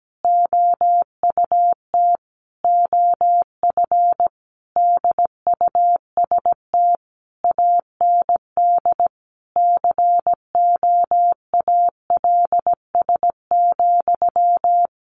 Morse code, heard as a keyed tone, about 17 wpm, 700 Hz. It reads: OUT OF DUST AND COALS,
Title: A Sound Of Thunder 17wpm